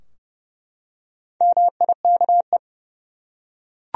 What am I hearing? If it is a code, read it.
Morse code, heard as a keyed tone, about 30 wpm, 700 Hz. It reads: MIKE
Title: USA Common Names - Max 4 Letters - 30wpm